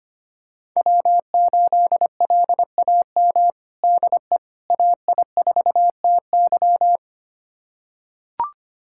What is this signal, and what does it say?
Morse code, heard as a keyed tone, about 25 wpm, 700 Hz. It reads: W8LAM DE AI4TY
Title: USA Callsign DE Callsign - 5 Characters - 25wpm